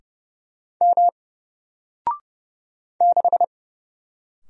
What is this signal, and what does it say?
Morse code, heard as a keyed tone, about 30 wpm, 700 Hz. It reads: M 6
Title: Single Letter-Number Alt 30wpm